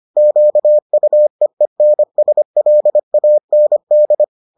Morse code, heard as a keyed tone, about 25 wpm, 600 Hz. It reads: QUEENSLAND